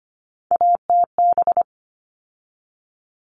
Morse code, heard as a keyed tone, about 25 wpm, 700 Hz. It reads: ATB
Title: All Cut Numbers - 3-Digit 25wpm